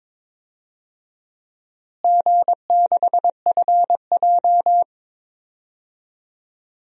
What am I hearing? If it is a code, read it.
Morse code, heard as a keyed tone, about 22 wpm, 700 Hz. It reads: G6FJ
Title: DXCC-100 Call Signs - 4 Characters - Repeated 3x - 22wpm